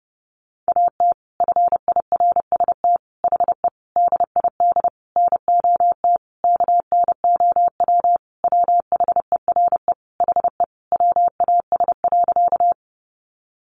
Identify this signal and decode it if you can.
Morse code, heard as a keyed tone, about 30 wpm, 700 Hz. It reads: AT FIRST HE DID NOT KNOW WHERE HE WAS.
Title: Sentences from Top 600 Words 30wpm